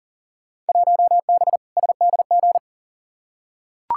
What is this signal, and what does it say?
Morse code, heard as a keyed tone, about 40 wpm, 700 Hz. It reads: 1B SDG
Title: Field Day Exchanges - 40wpm